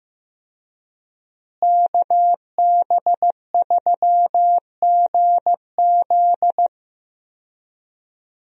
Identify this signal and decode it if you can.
Morse code, heard as a keyed tone, about 15 wpm, 700 Hz. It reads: KB3GZ
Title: USA Call Signs - 5 Characters - Repeated 2x - 15wpm